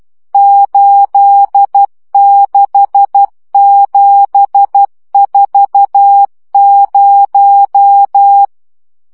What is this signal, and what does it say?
Morse code, heard as a keyed tone, about 12 wpm, 800 Hz. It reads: 86740
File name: morze vizsga 2011-2 10